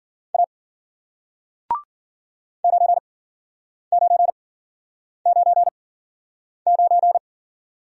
Morse code, heard as a keyed tone, about 51 wpm, spaced as for 9 wpm, 700 Hz. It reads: A 9999